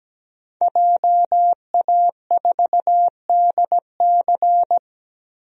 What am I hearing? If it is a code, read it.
Morse code, heard as a keyed tone, about 17 wpm, 700 Hz. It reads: JA4DC
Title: DXCC-100 Call Signs - 5 Characters - Repeated 2x - 17wpm